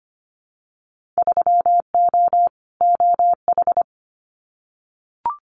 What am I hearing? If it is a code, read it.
Morse code, heard as a keyed tone, about 25 wpm, 700 Hz. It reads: 3O OH
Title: Winter Field Day Exchanges 25wpm